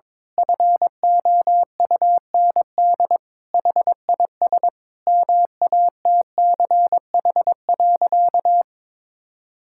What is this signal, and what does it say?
Morse code, heard as a keyed tone, about 22 wpm, 700 Hz. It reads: FOUND HIS MATCH.
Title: Sentences from Top 1600 Words 22wpm